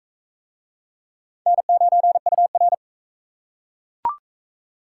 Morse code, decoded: N9UR

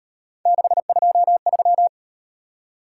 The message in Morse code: 623

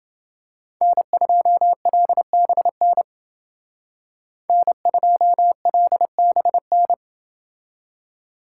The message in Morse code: N2LBN N2LBN

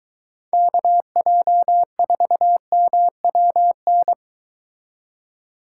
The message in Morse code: KJ4MWN

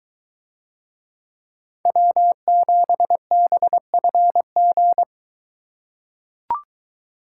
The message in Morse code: W7BFG

